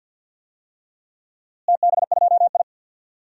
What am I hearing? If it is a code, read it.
Morse code, heard as a keyed tone, about 53 wpm, 700 Hz. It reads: TB2I
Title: DXCC-100 Call Signs - 4 Characters - Repeated 3x - 50wpm